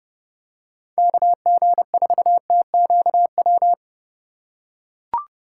Morse code, heard as a keyed tone, about 29 wpm, 700 Hz. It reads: KG4TQW